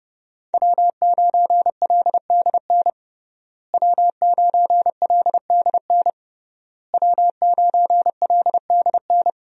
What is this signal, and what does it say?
Morse code, heard as a keyed tone, about 29 wpm, 700 Hz. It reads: W9LDN W9LDN W9LDN
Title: USA Call Signs - 5 Characters - Repeated 3x - 30wpm